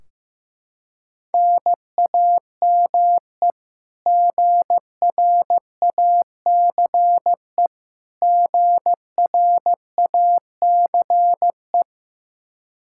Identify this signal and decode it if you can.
Morse code, heard as a keyed tone, about 15 wpm, 700 Hz. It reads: NAME GRACE GRACE
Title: Practice QSO - Exchange 1 - Beginning - 15wpm